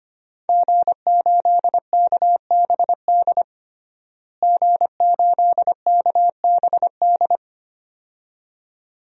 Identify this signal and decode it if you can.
Morse code, heard as a keyed tone, about 25 wpm, 700 Hz. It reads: G8KBD G8KBD
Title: DXCC-100 Call Signs - 5 Characters - Repeated 2x - 25wpm